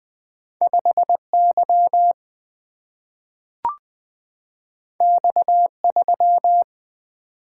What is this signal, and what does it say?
Morse code, heard as a keyed tone, about 20 wpm, 700 Hz. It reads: 5 Y X 3
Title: Single-Letter Single-Number 20wpm